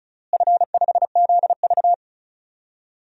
Morse code, heard as a keyed tone, about 36 wpm, 700 Hz. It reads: F5ZV
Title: DXCC-100 Call Signs - 4 Characters - Repeated 2x - 35wpm